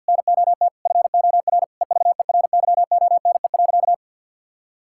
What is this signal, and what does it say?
Morse code, heard as a keyed tone, about 50 wpm, 700 Hz. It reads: NOT FOR EVERYONE.
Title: Sentences from Top 1000 Words - Review Part 5 - 50wpm